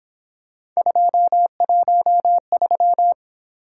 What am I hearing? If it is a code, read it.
Morse code, heard as a keyed tone, about 27 wpm, 700 Hz. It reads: 213
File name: Speed-Racing - Three Letter-Number Combinations - 22wpm